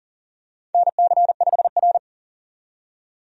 Morse code, heard as a keyed tone, about 40 wpm, 700 Hz. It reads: NC5R
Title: USA Call Signs - 4 Characters - 40wpm